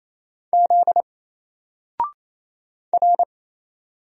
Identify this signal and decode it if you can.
Morse code, heard as a keyed tone, about 27 wpm, 700 Hz. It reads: Z R
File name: Single-Letter-Number-28wpm